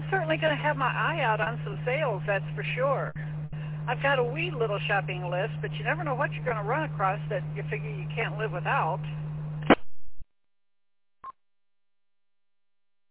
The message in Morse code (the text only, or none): none